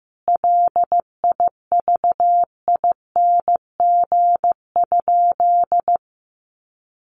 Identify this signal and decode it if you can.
Morse code, heard as a keyed tone, about 15 wpm, 700 Hz. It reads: LIVING?
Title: Sentences from Top 1100 Words 15wpm